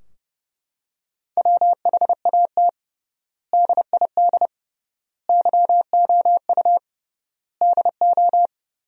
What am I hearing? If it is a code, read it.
Morse code, heard as a keyed tone, about 30 wpm, 700 Hz. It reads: WHAT DID YOU DO
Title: 3x Word-Spacing - Sentences from Top 100 Words - 30wpm